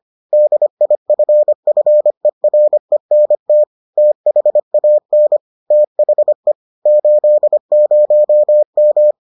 Morse code, heard as a keyed tone, about 25 wpm, 600 Hz. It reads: DIFFERENT THAN THE 80M